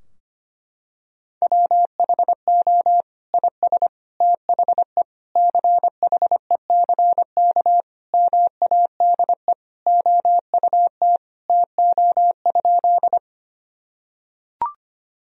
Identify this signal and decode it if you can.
Morse code, heard as a keyed tone, about 25 wpm, 700 Hz. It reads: WHO IS THE CHECK MADE OUT TO?
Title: Sentences from Top 1000 Words - Review Part 3 - 25wpm